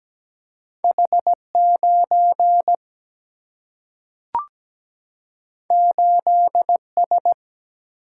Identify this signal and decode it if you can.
Morse code, heard as a keyed tone, about 17 wpm, 700 Hz. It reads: H9 8S
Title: Single-Letter Single-Number 17wpm